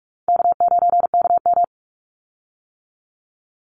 Morse code, heard as a keyed tone, about 44 wpm, 700 Hz. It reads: K9KM
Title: Call Signs - 45wpm